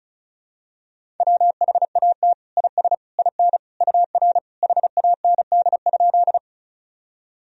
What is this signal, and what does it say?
Morse code, heard as a keyed tone, about 34 wpm, 700 Hz. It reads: WHAT IS IN UR HAND?
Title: Sentences from Top 200 Words and Abbreviations 35wpm